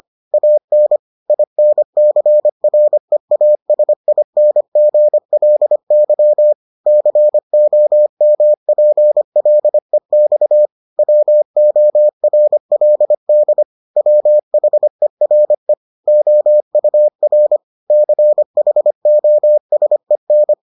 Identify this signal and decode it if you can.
Morse code, heard as a keyed tone, about 25 wpm, 600 Hz. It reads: AN INCREASINGLY COMPLEX WORLD WHERE OUR CHOSEN